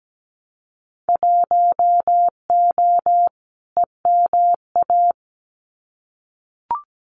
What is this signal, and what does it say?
Morse code, heard as a keyed tone, about 17 wpm, 700 Hz. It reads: 1O EMA
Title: Winter Field Day Exchanges 17wpm